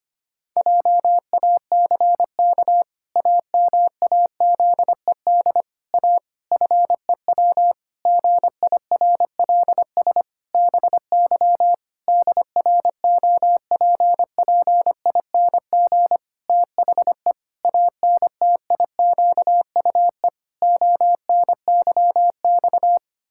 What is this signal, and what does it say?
Morse code, heard as a keyed tone, about 25 wpm, 700 Hz. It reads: JACK AMAZED A FEW GIRLS BY DROPPING THE ANTIQUE ONYX
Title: Pangram Challenge v2 - 25wpm